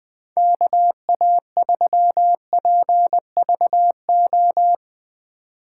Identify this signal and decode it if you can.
Morse code, heard as a keyed tone, about 20 wpm, 700 Hz. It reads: KA3PVO
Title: USA Call Signs - 6x Character Spacing - 20wpm